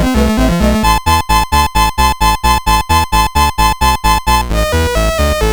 There is no code.